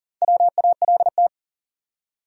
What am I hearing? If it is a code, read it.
Morse code, heard as a keyed tone, about 40 wpm, 700 Hz. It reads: WALT
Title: USA Common Names - Max 4 Letters - 40wpm